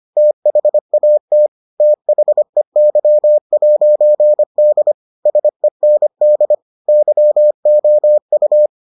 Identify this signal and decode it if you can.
Morse code, heard as a keyed tone, about 25 wpm, 600 Hz. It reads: THAT THEY'D SEND YOU